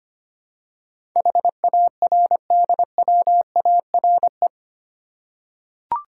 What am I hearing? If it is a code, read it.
Morse code, heard as a keyed tone, about 25 wpm, 700 Hz. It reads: HARDWARE
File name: Eight Letter Words - 25wpm